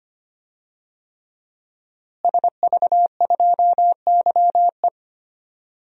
Morse code, heard as a keyed tone, about 25 wpm, 700 Hz. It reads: SV2YE
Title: DXCC-100 Call Signs - 5 Characters - Repeated 3x - 25wpm